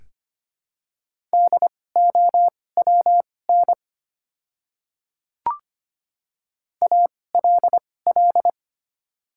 Morse code, heard as a keyed tone, about 25 wpm, spaced as for 18 wpm, 700 Hz. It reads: DOWN ALL
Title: 2x Character Spacing - Top 100 Words - 25wpm